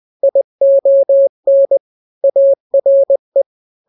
Morse code, heard as a keyed tone, about 20 wpm, 550 Hz. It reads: ION ARE
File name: n-grams / 3-grams 20 18 WPM